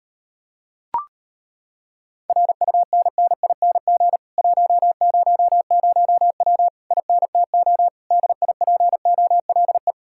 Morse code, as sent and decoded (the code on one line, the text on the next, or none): .-. ..- -. -. .. -. --. / .---- ----- ----- .-- / .. -. - --- / -.. .. .--. --- .-.. .
RUNNING 100W INTO DIPOLE